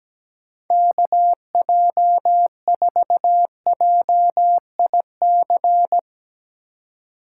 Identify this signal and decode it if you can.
Morse code, heard as a keyed tone, about 17 wpm, 700 Hz. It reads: KJ4JIC
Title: USA Call Signs - 6 Characters - 3x Character Spacing - 17wpm